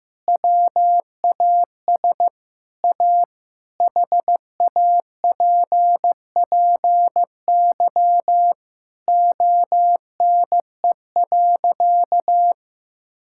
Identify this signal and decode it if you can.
Morse code, heard as a keyed tone, about 15 wpm, 700 Hz. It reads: WAS A HAPPY ONE.
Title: Sentences from Top 1000 Words - Review Part 5 - 15wpm